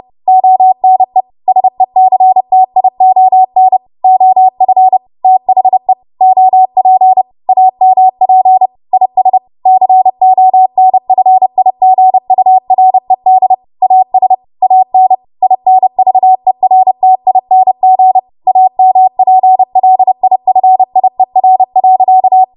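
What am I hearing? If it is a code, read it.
Morse code, heard as a keyed tone, about 30 wpm, 750 Hz. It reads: ONE SECTION OF THE OP AMP IS CONFIGURED AS AN INVERTING AMPLIFIER.
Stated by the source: W1AW Web Code Practice Files, 30 WPM run from ARRL; text from April 2023 QST.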